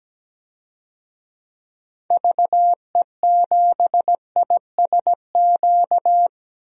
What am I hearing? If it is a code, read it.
Morse code, heard as a keyed tone, about 17 wpm, 700 Hz. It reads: VE7ISQ
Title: DXCC-100 Call Signs - 6 Characters - 17wpm